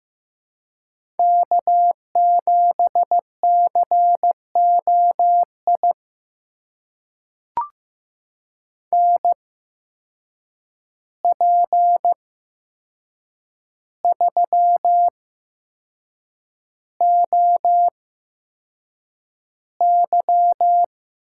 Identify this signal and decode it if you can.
Morse code, heard as a keyed tone, about 15 wpm, spaced as for 6 wpm, 700 Hz. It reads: K7COI NP3OY
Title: USA Call Signs - 8x Character Spacing - 15wpm